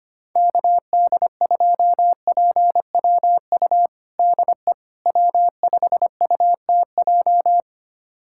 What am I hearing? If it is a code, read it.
Morse code, heard as a keyed tone, about 25 wpm, 700 Hz. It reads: KD2PWU DE W5UTJ
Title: USA Callsign DE Callsign 25wpm